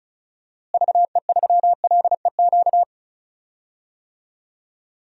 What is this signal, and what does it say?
Morse code, heard as a keyed tone, about 36 wpm, 700 Hz. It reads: VE3LEQ
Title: DXCC-100 Call Signs - 6 Characters - Repeated 3x - 35wpm